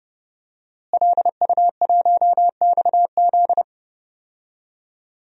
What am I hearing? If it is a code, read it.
Morse code, heard as a keyed tone, about 31 wpm, 700 Hz. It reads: LU1XZ
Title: DXCC-100 Call Signs - 5 Characters - Repeated 2x - 30wpm